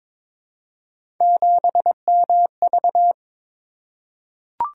Morse code, heard as a keyed tone, about 22 wpm, 700 Hz. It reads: 7MV
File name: Three Letter-Number Combinations - 22wpm